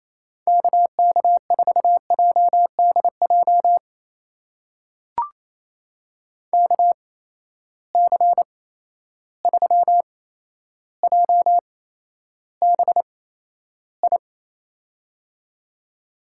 Morse code, heard as a keyed tone, about 28 wpm, spaced as for 12 wpm, 700 Hz. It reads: KK4JDJ KC3JBI
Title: USA Call Signs - 6 Characters - 8x Character Spacing - 28wpm